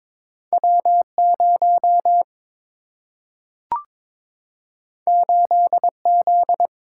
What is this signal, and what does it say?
Morse code, heard as a keyed tone, about 22 wpm, 700 Hz. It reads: W 0 8 Z
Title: Single-Letter Single-Number 22wpm